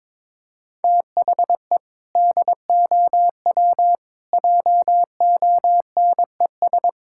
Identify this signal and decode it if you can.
Morse code, heard as a keyed tone, about 22 wpm, 700 Hz. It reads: THE DOW JONES
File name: practice / 22wpm / Build A Phrase - 5-Word Phrases - 22wpm